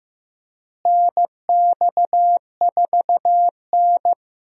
Morse code, heard as a keyed tone, about 15 wpm, 700 Hz. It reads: NX4N
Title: USA Call Signs - 4 Characters - Repeated 3x - 15wpm